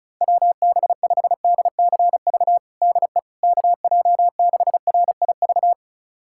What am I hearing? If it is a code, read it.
Morse code, heard as a keyed tone, about 34 wpm, 700 Hz. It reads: WB5DCV DE KJ6RIV
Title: USA Callsign DE Callsign - 6 Characters - 35wpm